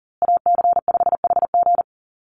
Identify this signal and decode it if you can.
Morse code, heard as a keyed tone, about 40 wpm, 700 Hz. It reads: AC5HG
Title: USA Call Signs Repeated 2x 40wpm